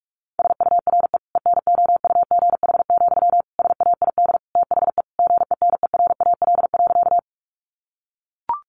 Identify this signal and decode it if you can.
Morse code, heard as a keyed tone, about 44 wpm, 700 Hz. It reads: SURE ENOUGH, SAID THE GENERAL.
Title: Sentences from Top 600 Words 45wpm